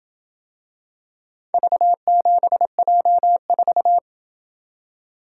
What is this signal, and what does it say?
Morse code, heard as a keyed tone, about 27 wpm, 700 Hz. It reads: V7J4